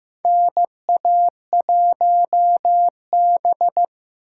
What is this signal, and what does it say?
Morse code, heard as a keyed tone, about 15 wpm, 700 Hz. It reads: NA1B
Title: USA Call Signs - 4 Characters - 3x Character Spacing - 15wpm